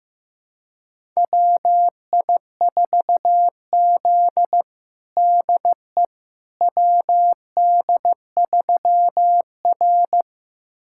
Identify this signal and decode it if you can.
Morse code, heard as a keyed tone, about 15 wpm, 700 Hz. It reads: WI4Z DE WD3R
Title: USA Callsign DE Callsign - 4 Characters - 15wpm